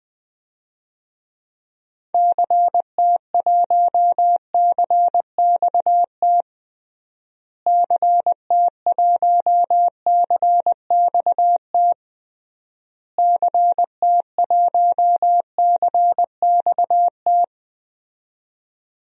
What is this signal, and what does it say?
Morse code, heard as a keyed tone, about 20 wpm, 700 Hz. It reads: CT1CXT CT1CXT CT1CXT
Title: DXCC-100 Call Signs - 6 Characters - Repeated 3x - 20wpm